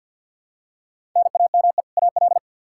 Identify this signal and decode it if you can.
Morse code, heard as a keyed tone, about 50 wpm, 700 Hz. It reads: NAME AL
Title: Practice QSO - Exchange 1 - Beginning - 50wpm